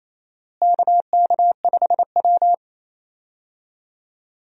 Morse code, heard as a keyed tone, about 28 wpm, 700 Hz. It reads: KK5W
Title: USA Call Signs - 4 Characters - Repeated 2x - 28wpm